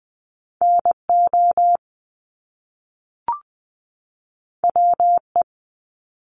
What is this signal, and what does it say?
Morse code, heard as a keyed tone, about 20 wpm, 700 Hz. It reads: NO WE
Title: Top Two Letter Combinations in Words 20wpm